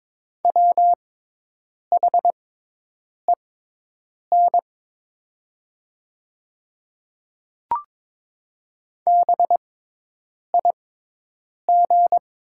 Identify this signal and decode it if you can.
Morse code, heard as a keyed tone, about 22 wpm, spaced as for 8 wpm, 700 Hz. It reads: WHEN BIG
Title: 6x Character Spacing - Top 200 Words - 22wpm